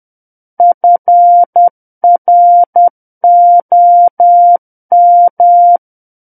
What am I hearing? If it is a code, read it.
Morse code, heard as a keyed tone, about 10 wpm, 700 Hz. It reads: FROM
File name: Book One / 01-Chapter 1 - 10 WPM